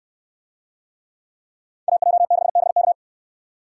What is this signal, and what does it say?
Morse code, heard as a keyed tone, about 69 wpm, 700 Hz. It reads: AJ6BK